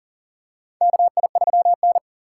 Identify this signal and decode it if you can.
Morse code, heard as a keyed tone, about 40 wpm, 700 Hz. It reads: KI3N
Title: USA Call Signs - 3x Character Spacing - 40wpm